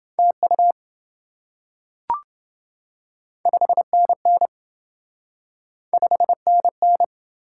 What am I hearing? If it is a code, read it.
Morse code, heard as a keyed tone, about 27 wpm, 700 Hz. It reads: TU 5NN 5NN